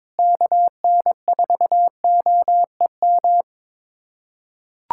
Morse code, heard as a keyed tone, about 21 wpm, 700 Hz. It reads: KN4OEM
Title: USA Callsign DE Callsign 22wpm